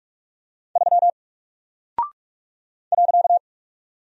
Morse code, heard as a keyed tone, about 44 wpm, 700 Hz. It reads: 3 .
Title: Single Character (Letter:Number:Punctuation:Prosign) 45wpm